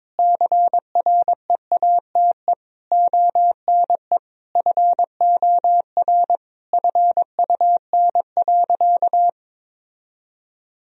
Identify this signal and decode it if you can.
Morse code, heard as a keyed tone, about 22 wpm, 700 Hz. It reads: CREATE ONE FOR FUN.
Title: Sentences from Top 1000 Words - Review Part 3 - 22wpm